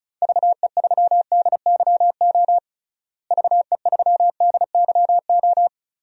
Morse code, heard as a keyed tone, about 36 wpm, 700 Hz. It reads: VE3DYO VE3DYO
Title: DXCC-100 Call Signs - 6 Characters - Repeated 3x - 35wpm